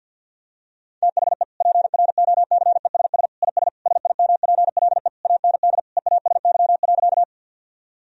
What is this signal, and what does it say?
Morse code, heard as a keyed tone, about 46 wpm, 700 Hz. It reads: THE PROCESS IS SIMPLE AND EASY.
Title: Sentences from Top 1000 Words - Review Part 3 - 50wpm